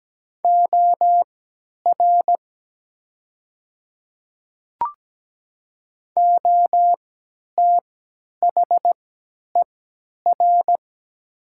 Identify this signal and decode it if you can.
Morse code, heard as a keyed tone, about 17 wpm, spaced as for 10 wpm, 700 Hz. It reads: OR OTHER
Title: 3x Character Spacing - Top 100 Words - 17wpm